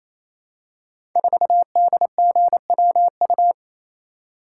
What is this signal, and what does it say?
Morse code, heard as a keyed tone, about 28 wpm, 700 Hz. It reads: 4DGWU
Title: Code Groups - 5 Letter:Numbers - 28wpm